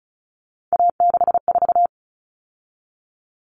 Morse code, heard as a keyed tone, about 34 wpm, 700 Hz. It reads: A64
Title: Common Cut Numbers - 3-Digit 35wpm